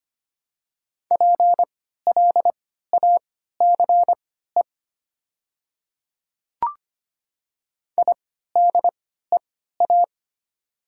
Morse code, heard as a keyed tone, about 25 wpm, spaced as for 14 wpm, 700 Hz. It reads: PLACE IDEA
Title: 3x Character Spacing - Top 500 Words - 25wpm